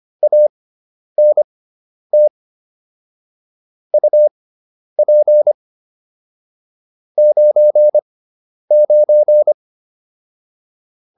Morse code, heard as a keyed tone, about 25 wpm, spaced as for 10 wpm, 600 Hz. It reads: ANT UP 99
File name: QSO304 10